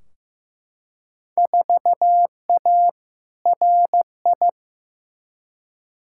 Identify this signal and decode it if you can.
Morse code, heard as a keyed tone, about 15 wpm, 700 Hz. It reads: 4A RI
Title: Field Day Exchanges Encore - 15wpm